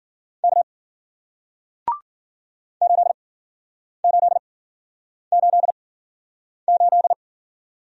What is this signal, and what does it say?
Morse code, heard as a keyed tone, about 54 wpm, spaced as for 9 wpm, 700 Hz. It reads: K 8888